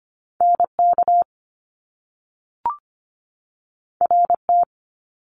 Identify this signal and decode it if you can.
Morse code, heard as a keyed tone, about 25 wpm, 700 Hz. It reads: NK RT
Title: Top Two Letter Combinations in Words 25wpm